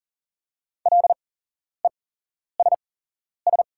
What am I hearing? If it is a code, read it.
Morse code, heard as a keyed tone, about 40 wpm, spaced as for 11 wpm, 690 Hz. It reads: LESS
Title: 8x Character Spacing - Top 500 Words - 40wpm